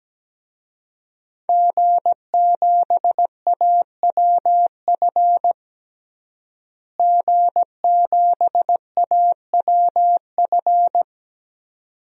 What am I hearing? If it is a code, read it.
Morse code, heard as a keyed tone, about 17 wpm, 700 Hz. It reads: G7AWF G7AWF
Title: DXCC-100 Call Signs - 5 Characters - Repeated 2x - 17wpm